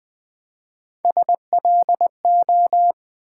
Morse code, heard as a keyed tone, about 20 wpm, 700 Hz. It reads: SLO
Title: Build A Word - 3 Letters At A Time - 6-Letter Words - 20wpm